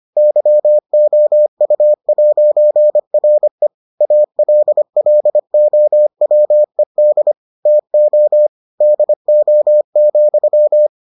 Morse code, heard as a keyed tone, about 25 wpm, 600 Hz. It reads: YOU'RE ALLOWED TO DO,